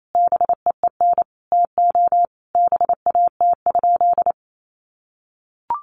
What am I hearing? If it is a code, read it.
Morse code, heard as a keyed tone, about 28 wpm, 700 Hz. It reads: BEEN TO BAT?
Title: Sentences from Top 1500 Words 28wpm